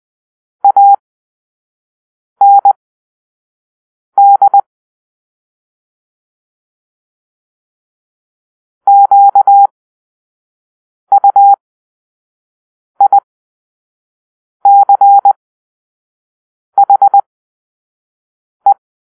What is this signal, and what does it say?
Morse code, heard as a keyed tone, about 20 wpm, spaced as for 5 wpm, 800 Hz. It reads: AND QUICHE
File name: arrl / Grp-5 Drill-4